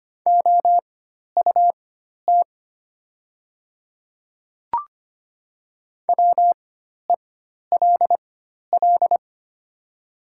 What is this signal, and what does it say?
Morse code, heard as a keyed tone, about 25 wpm, spaced as for 12 wpm, 690 Hz. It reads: OUT WELL
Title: 4x Character Spacing - Top 300 Words - 25wpm